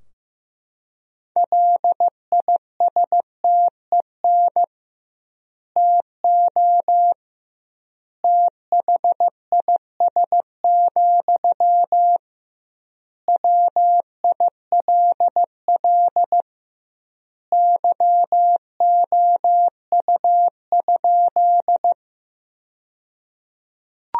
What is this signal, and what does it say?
Morse code, heard as a keyed tone, about 15 wpm, 700 Hz. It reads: LISTEN TO THIS, WILL YOU?
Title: 2x Word-Spacing - Sentences from Top 400 Words - 15wpm